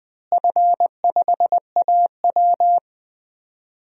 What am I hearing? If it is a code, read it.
Morse code, heard as a keyed tone, about 20 wpm, 700 Hz. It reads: F5AW
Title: DXCC-100 Call Signs - 4 Characters - Repeated 2x - 20wpm